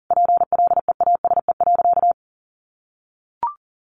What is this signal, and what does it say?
Morse code, heard as a keyed tone, about 40 wpm, 700 Hz. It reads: PLEASE.
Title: Sentences from Top 1500 Words 40wpm